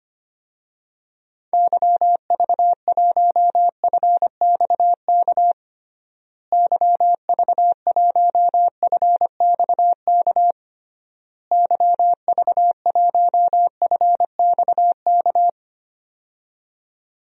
Morse code, heard as a keyed tone, about 25 wpm, 700 Hz. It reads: YV1FXK YV1FXK YV1FXK
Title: DXCC-100 Call Signs - 6 Characters - Repeated 3x - 25wpm